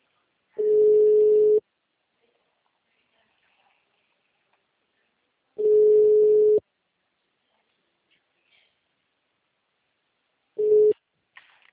Ringback tone (Europe, 425 Hz).